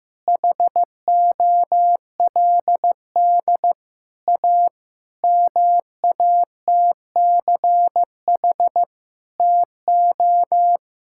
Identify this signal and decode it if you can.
Morse code, heard as a keyed tone, about 15 wpm, 700 Hz. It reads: HOLD A MATCH TO
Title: Sentences from Top 1600 Words 15wpm